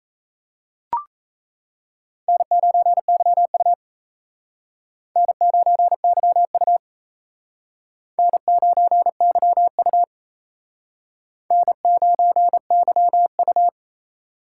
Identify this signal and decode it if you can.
Morse code, heard as a keyed tone, about 35 wpm, 700 Hz. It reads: N9YU N9YU N9YU N9YU